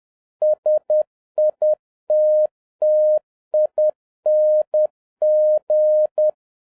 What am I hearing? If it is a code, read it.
Morse code, heard as a keyed tone, about 10 wpm, 610 Hz. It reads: SITTING
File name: 2011-07-mp3 / n46 JOHN K3TN 10wpm